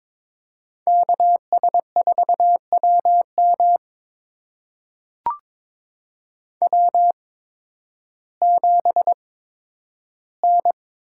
Morse code, heard as a keyed tone, about 22 wpm, spaced as for 12 wpm, 700 Hz. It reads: KS4WM W7N
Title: USA Call Signs - 8x Character Spacing - 22wpm